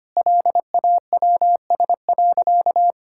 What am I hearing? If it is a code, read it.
Morse code, heard as a keyed tone, about 25 wpm, 700 Hz. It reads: LAWS.
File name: practice / 25wpm / Sentences with 2-Word Phrases - 25wpm